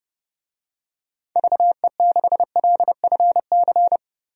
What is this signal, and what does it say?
Morse code, heard as a keyed tone, about 30 wpm, 700 Hz. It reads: VE6LFC
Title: DXCC-100 Call Signs - 6 Characters - 30wpm